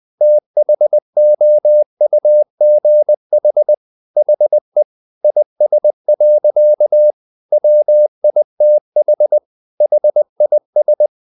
Morse code, heard as a keyed tone, about 20 wpm, 600 Hz. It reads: THOUGH HE IS. WITH HIS
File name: the old man and the sea / the old man and the sea part 1 20wpm